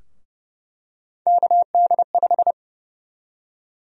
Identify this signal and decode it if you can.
Morse code, heard as a keyed tone, about 30 wpm, 700 Hz. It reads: KD5
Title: USA Call Sign Prefixes - 30wpm